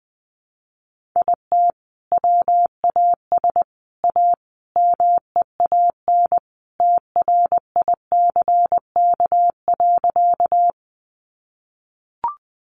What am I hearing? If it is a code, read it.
Morse code, heard as a keyed tone, about 20 wpm, 700 Hz. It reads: IT WAS A MEAN TRICK.
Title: Sentences from Top 1500 Words 20wpm